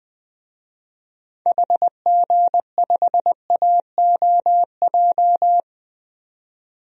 Morse code, heard as a keyed tone, about 21 wpm, 700 Hz. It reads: HG5AOJ
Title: DXCC-100 Call Signs - 6 Characters - 20wpm